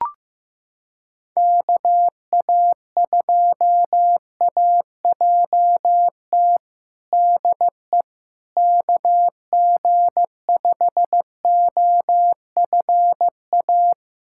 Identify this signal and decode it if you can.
Morse code, heard as a keyed tone, about 16 wpm, 700 Hz. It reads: KA2AJT DE KG5OFA
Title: USA Callsign DE Callsign - 6 Characters - 15wpm